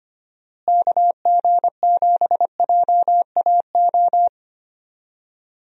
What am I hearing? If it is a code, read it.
Morse code, heard as a keyed tone, about 25 wpm, 700 Hz. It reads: KG7JAO
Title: USA Call Signs - 6 Characters - Repeated 3x - 25wpm